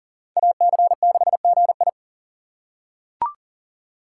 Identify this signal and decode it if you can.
Morse code, heard as a keyed tone, about 37 wpm, 700 Hz. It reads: AC6GI
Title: USA Call Signs - 5 Characters - 2x Character Spacing - 40wpm